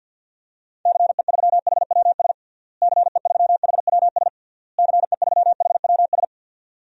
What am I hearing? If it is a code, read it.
Morse code, heard as a keyed tone, about 51 wpm, 700 Hz. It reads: CE3HWS CE3HWS CE3HWS
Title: DXCC-100 Call Signs - 6 Characters - Repeated 3x - 50wpm